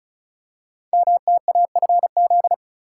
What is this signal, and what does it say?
Morse code, heard as a keyed tone, about 36 wpm, 700 Hz. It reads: MTAFZ
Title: Code Groups - 5 Letters - 35wpm